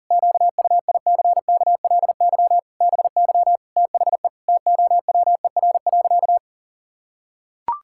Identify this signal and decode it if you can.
Morse code, heard as a keyed tone, about 40 wpm, 700 Hz. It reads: QUICKLY BY THE TOWER.
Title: Pangram Challenge v2 - 40wpm